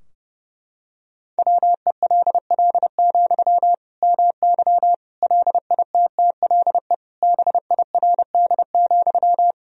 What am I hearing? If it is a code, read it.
Morse code, heard as a keyed tone, about 30 wpm, 700 Hz. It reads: WELL, MY LITTLE BIRD,
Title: Sentences from Top 1000 Words - Review Part 2 - 30wpm